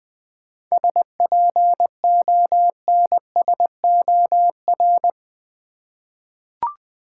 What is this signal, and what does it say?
Morse code, heard as a keyed tone, about 20 wpm, 700 Hz. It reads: SPONSOR
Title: 7 Letter Words - First 3 Letters Given - 20wpm